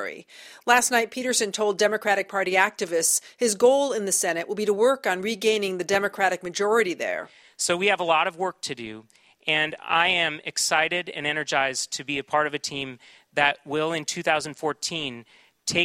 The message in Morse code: none

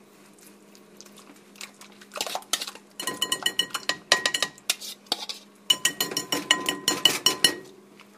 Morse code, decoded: none

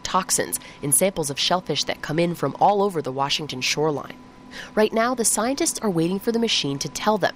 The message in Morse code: none

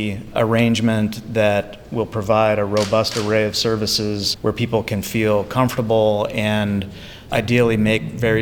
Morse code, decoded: none

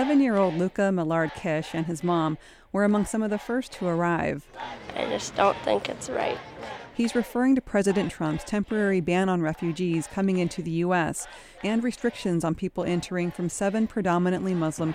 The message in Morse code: none

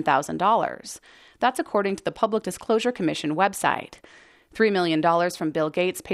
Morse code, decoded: none

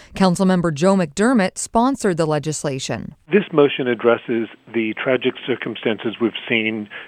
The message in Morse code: none

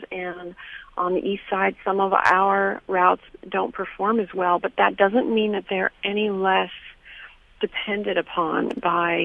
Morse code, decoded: none

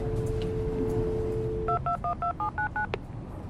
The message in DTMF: 2212799

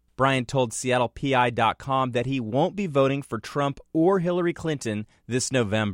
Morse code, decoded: none

